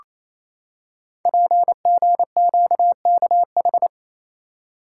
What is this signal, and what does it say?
Morse code, heard as a keyed tone, about 29 wpm, 690 Hz. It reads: PGQKH
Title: Code Groups - 5 Letters - 28wpm